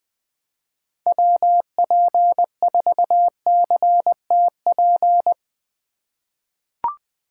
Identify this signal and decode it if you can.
Morse code, heard as a keyed tone, about 20 wpm, 700 Hz. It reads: WP4CTP
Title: USA Call Signs - 6 Characters - 6x Character Spacing - 20wpm